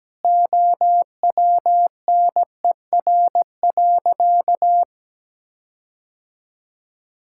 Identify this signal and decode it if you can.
Morse code, heard as a keyed tone, about 17 wpm, 700 Hz. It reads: OWNER.
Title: Sentences from Top 1600 Words 17wpm